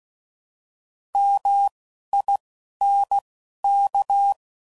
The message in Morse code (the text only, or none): MINK